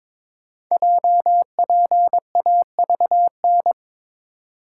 Morse code, heard as a keyed tone, about 22 wpm, 700 Hz. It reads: JPAVN
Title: Code Groups - 5 Letters - 22wpm